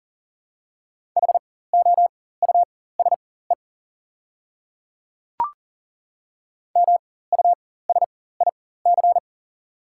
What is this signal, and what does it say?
Morse code, heard as a keyed tone, about 40 wpm, spaced as for 19 wpm, 700 Hz. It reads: HOUSE MUSIC
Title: 4x Character Spacing - Top 300 Words - 40wpm